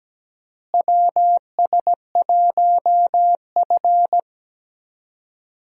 Morse code, heard as a keyed tone, about 17 wpm, 700 Hz. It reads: WS1F
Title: USA Call Signs - 4 Characters - Repeated 2x - 17wpm